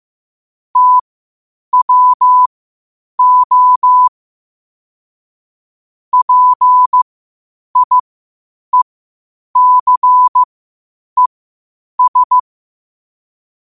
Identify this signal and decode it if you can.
Morse code, heard as a keyed tone, about 15 wpm, spaced as for 8 wpm, 1000 Hz. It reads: TWO PIECES
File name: qotdfiles / 8wpm